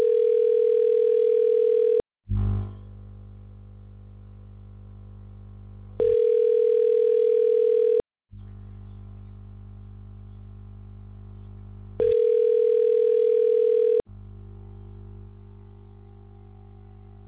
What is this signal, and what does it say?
Ringback tone (North America), 440 + 480 Hz.